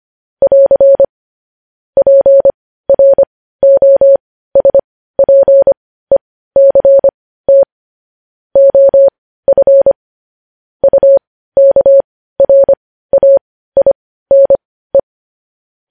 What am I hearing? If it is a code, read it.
Morse code, heard as a keyed tone, about 25 wpm, spaced as for 15 wpm, 550 Hz. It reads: <AR> PROSPECT OF UKRAINE
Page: news headlines in morse code at 15 wpm